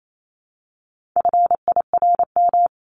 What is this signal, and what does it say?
Morse code, heard as a keyed tone, about 28 wpm, 700 Hz. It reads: FIRM
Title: Top 1251-1500 Words 28wpm